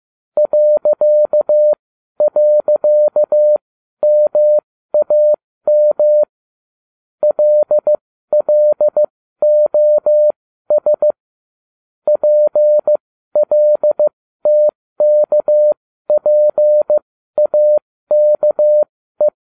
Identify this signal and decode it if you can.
Morse code, heard as a keyed tone, about 15 wpm, spaced as for 12 wpm, 600 Hz. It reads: ..MAM LLOS PLTKPAKE